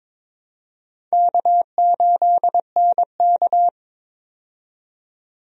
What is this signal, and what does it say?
Morse code, heard as a keyed tone, about 22 wpm, 700 Hz. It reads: K8NK
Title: USA Call Signs - 4 Characters - 22wpm